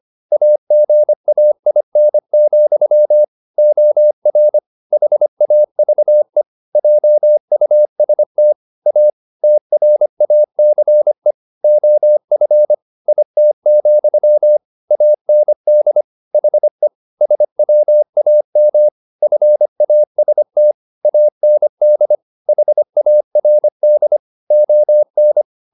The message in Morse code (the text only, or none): AGAIN, OR HAVE JUST A TRACE OF IT, AND HE SWAM FAST AND HARD ON